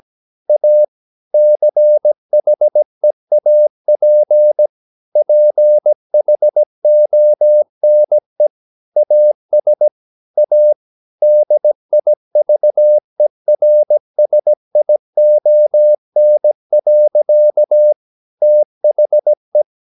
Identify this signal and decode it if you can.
Morse code, heard as a keyed tone, about 17 wpm, 600 Hz. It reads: A CHEAP PHONE AS A DIVERSION. THE
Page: Romanen Never af Ken Follet på engelsk.